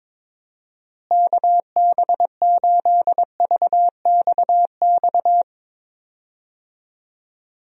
Morse code, heard as a keyed tone, about 21 wpm, 700 Hz. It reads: KB8VXX
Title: USA Call Signs - 6 Characters - Repeated 2x - 22wpm